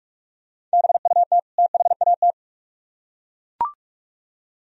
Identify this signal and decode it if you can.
Morse code, heard as a keyed tone, about 39 wpm, 700 Hz. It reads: BUT THAT